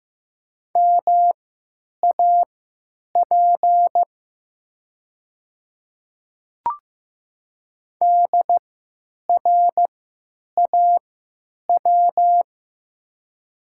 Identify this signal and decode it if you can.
Morse code, heard as a keyed tone, about 15 wpm, spaced as for 9 wpm, 700 Hz. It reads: MAP DRAW
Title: 3x Character Spacing - Top 500 Words - 15wpm